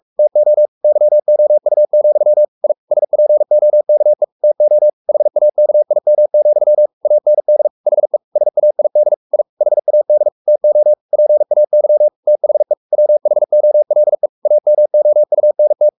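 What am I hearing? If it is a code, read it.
Morse code, heard as a keyed tone, about 44 wpm, 600 Hz. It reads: TO YOU, I SPOKE TO HAKIM, AND HE SAID I HAD TO PAY THE WHOLE AMOUNT
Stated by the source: Morse code at 43 wpm sending Ken Follett's novel Never in English, from CW med Gnister.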